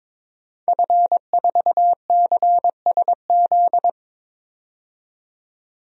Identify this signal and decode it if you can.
Morse code, heard as a keyed tone, about 22 wpm, 700 Hz. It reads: F4CSZ
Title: DXCC-100 Call Signs - 5 Characters - 22wpm